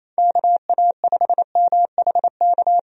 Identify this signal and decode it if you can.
Morse code, heard as a keyed tone, about 27 wpm, 700 Hz. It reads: KA5MHK
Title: USA Call Signs - 6 Characters - 28wpm